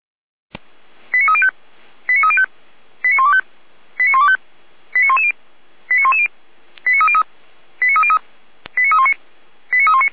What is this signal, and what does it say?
Five-tone selective call (ZVEI), heard as quick runs of five tones, tones of about 70 ms. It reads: 89305 89305 89125 89125 89110 89110 89303 89303 89319 89319